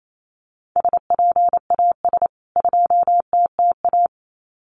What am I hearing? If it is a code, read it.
Morse code, heard as a keyed tone, about 28 wpm, 700 Hz. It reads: SPAS 2TTA
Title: CWT Exchange 28wpm